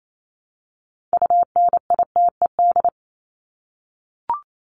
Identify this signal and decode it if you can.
Morse code, heard as a keyed tone, about 28 wpm, 700 Hz. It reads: UNITED
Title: ED Suffix - Top 100 Words with Base Words 28wpm